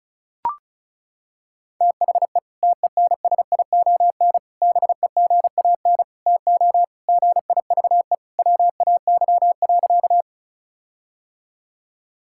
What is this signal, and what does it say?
Morse code, heard as a keyed tone, about 34 wpm, 700 Hz. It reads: THE TENSION BEGAN TO GIVE WAY.
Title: Sentences from Top 1600 Words 35wpm